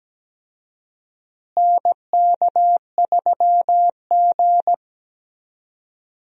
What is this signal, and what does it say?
Morse code, heard as a keyed tone, about 17 wpm, 700 Hz. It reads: NK3G
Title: USA Call Signs - 4 Characters - Repeated 2x - 17wpm